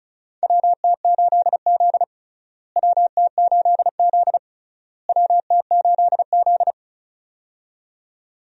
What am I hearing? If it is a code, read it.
Morse code, heard as a keyed tone, about 35 wpm, 700 Hz. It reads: WT8Z WT8Z WT8Z
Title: USA Call Signs - 4 Characters - Repeated 3x - 35wpm